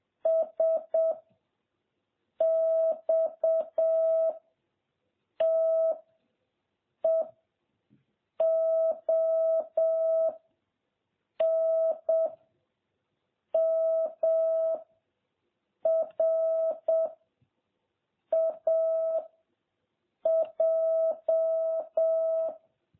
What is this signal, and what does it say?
Morse code, heard as a keyed tone, about 7 wpm, 660 Hz. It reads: S X T E O N M R A J
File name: jh8cbh / 2021cw